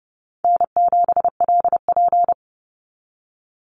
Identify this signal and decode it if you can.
Morse code, heard as a keyed tone, about 30 wpm, 700 Hz. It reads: N7LP
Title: NA SOTA Chaser Callsigns 30wpm